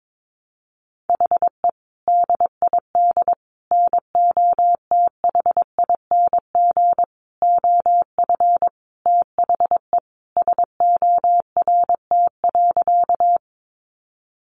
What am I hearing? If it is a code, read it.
Morse code, heard as a keyed tone, about 22 wpm, 700 Hz. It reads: HE DID NOTHING OF THE SORT.
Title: Sentences from Top 1300 Words 22wpm